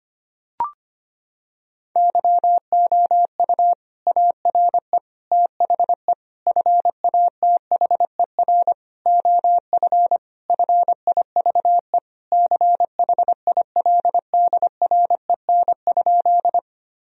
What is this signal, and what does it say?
Morse code, heard as a keyed tone, about 25 wpm, 700 Hz. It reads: YOU ARE THE FATHER OF FIVE CHILDREN?
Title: Sentences from Top 1000 Words - Review Part 2 - 25wpm